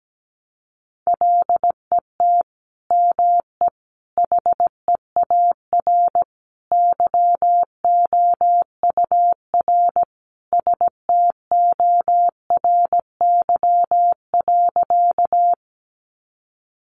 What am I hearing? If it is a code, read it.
Morse code, heard as a keyed tone, about 17 wpm, 700 Hz. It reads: LET ME HEAR YOUR STORY.
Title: Sentences from Top 300 Words 17wpm